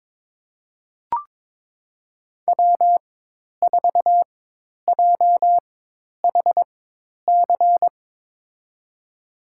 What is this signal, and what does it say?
Morse code, heard as a keyed tone, about 22 wpm, spaced as for 10 wpm, 700 Hz. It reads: W4JHC
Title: USA Call Signs - 5 Characters - 4x Character Spacing - 22wpm